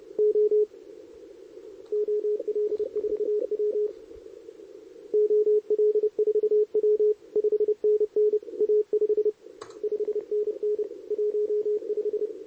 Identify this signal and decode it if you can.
Morse code, heard as a keyed tone, about 29 wpm, 440 Hz, with radio noise. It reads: O OL4W OL4W 5NN A5 5NN 15
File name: CQWWCW2023 / SP